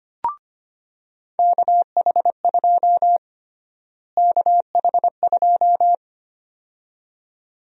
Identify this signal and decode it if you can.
Morse code, heard as a keyed tone, about 25 wpm, 700 Hz. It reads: KH2 KH2
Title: USA Call Sign Prefixes - Repeated 2x - 25wpm